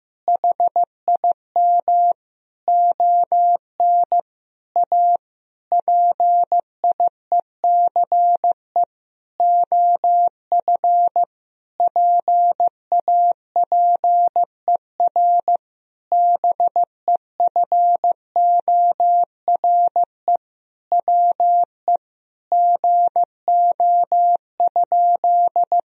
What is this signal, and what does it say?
Morse code, heard as a keyed tone, about 15 wpm, 700 Hz. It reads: HIM ON A PIECE OF PAPER BEFORE WE GO?
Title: Sentences from Top 1000 Words - Review Part 2 - 15wpm